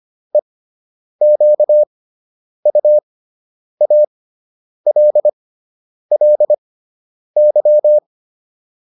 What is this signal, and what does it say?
Morse code, heard as a keyed tone, about 25 wpm, spaced as for 9 wpm, 600 Hz. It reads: EQUALLY